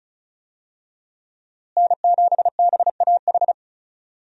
Morse code, heard as a keyed tone, about 34 wpm, 700 Hz. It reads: N7BAH
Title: USA Call Signs - 5 Characters - Repeated 2x - 35wpm